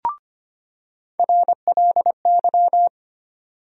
Morse code, heard as a keyed tone, about 25 wpm, 700 Hz. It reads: RLY
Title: USA Call Sign Postfixes - 3 Letters 25wpm